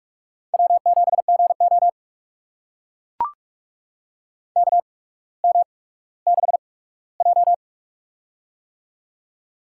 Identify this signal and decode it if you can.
Morse code, heard as a keyed tone, about 44 wpm, spaced as for 19 wpm, 700 Hz. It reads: W7GO KM6J
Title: USA Call Signs - 8x Character Spacing - 45wpm